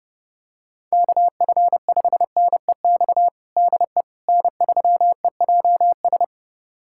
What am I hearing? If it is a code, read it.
Morse code, heard as a keyed tone, about 32 wpm, 700 Hz. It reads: KF5NEX DE N3EJS
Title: USA Callsign DE Callsign 30wpm